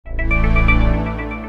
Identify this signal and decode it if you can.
There is no code.